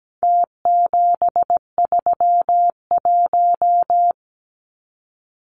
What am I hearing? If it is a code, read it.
Morse code, heard as a keyed tone, about 17 wpm, 700 Hz. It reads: T731
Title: Numbers - Contest Style - 4-Digit 17wpm